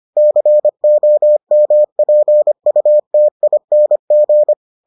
Morse code, heard as a keyed tone, about 25 wpm, 600 Hz. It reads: COMPUTING